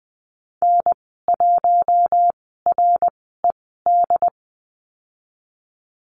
Morse code, heard as a keyed tone, about 20 wpm, spaced as for 14 wpm, 700 Hz. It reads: N1RED
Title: 069 Lesson 12g - Letter d - Call Signs 20wpm